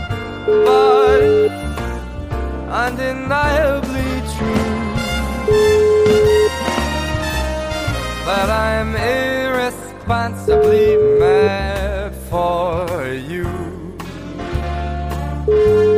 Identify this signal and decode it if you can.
Ringback tone (Europe, 425 Hz), with noise.